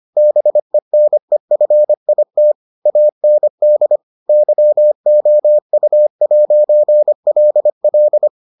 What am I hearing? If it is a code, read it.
Morse code, heard as a keyed tone, about 25 wpm, 600 Hz. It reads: BENEFIT AND YOU'LL